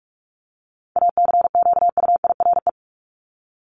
Morse code, heard as a keyed tone, about 44 wpm, 700 Hz. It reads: ACQUIRE
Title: Top 1501-1750 Words 45wpm